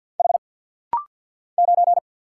Morse code, heard as a keyed tone, about 50 wpm, 700 Hz. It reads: H 9
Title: Single Letter-Number - Rapid-Fire 50wpm